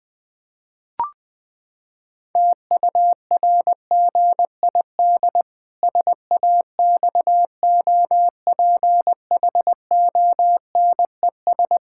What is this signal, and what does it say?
Morse code, heard as a keyed tone, about 20 wpm, 700 Hz. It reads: TURGID SAXOPHONES
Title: Pangram Challenge v2 - 20wpm